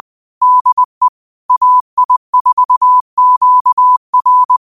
Morse code, heard as a keyed tone, about 20 wpm, 1000 Hz. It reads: DE AI4QR
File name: qotdfiles / 20wpm